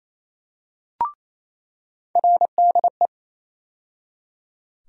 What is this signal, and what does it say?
Morse code, heard as a keyed tone, about 27 wpm, 700 Hz. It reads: RDE
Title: USA Call Sign Postfixes - 3 Letters 28wpm